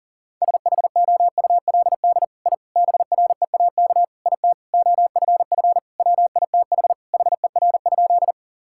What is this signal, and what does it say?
Morse code, heard as a keyed tone, about 37 wpm, 700 Hz. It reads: SHOULD I BREAK IT OFF WITH HER?
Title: Sentences from Top 1000 Words - Review Part 6 - 40wpm